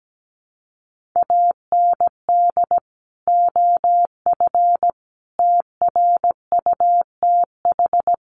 Morse code, heard as a keyed tone, about 17 wpm, 700 Hz. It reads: AND OF TRUTH
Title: Sentences from Top 1500 Words 17wpm